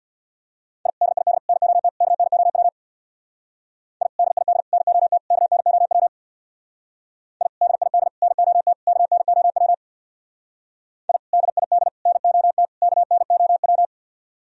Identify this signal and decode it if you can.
Morse code, heard as a keyed tone, about 64 wpm, 700 Hz. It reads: I DID NOT KNOW I DID NOT KNOW I DID NOT KNOW I DID NOT KNOW